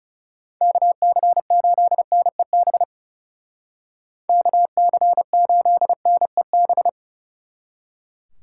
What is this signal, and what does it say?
Morse code, heard as a keyed tone, about 36 wpm, 700 Hz. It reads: KC8NEB KC8NEB
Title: Speed Racing - USA 2x3 Call Signs 30wpm